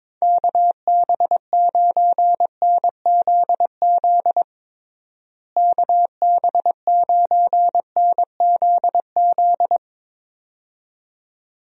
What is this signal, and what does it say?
Morse code, heard as a keyed tone, about 21 wpm, 700 Hz. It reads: KB9NZZ KB9NZZ
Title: USA Call Signs - 6 Characters - Repeated 2x - 22wpm